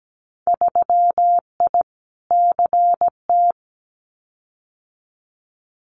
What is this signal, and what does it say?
Morse code, heard as a keyed tone, about 17 wpm, 690 Hz. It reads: 3I CT
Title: Winter Field Day Exchanges 17wpm